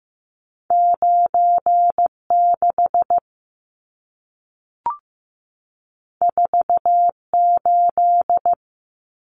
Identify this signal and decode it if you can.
Morse code, heard as a keyed tone, about 15 wpm, 700 Hz. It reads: 96 48
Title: Numbers - 2-Digit 15wpm